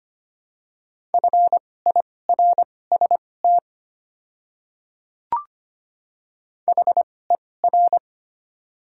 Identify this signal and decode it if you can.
Morse code, heard as a keyed tone, about 25 wpm, spaced as for 18 wpm, 700 Hz. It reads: FIRST HER
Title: 2x Character Spacing - Top 100 Words - 25wpm